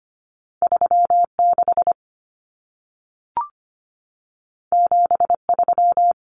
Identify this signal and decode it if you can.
Morse code, heard as a keyed tone, about 25 wpm, 700 Hz. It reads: 36 73
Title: Numbers - 2-Digit 25wpm